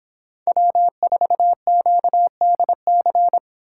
Morse code, heard as a keyed tone, about 25 wpm, 700 Hz. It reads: W4QDC